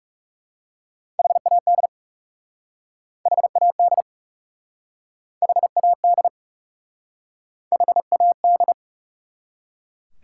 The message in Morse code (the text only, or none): HAD HAD HAD HAD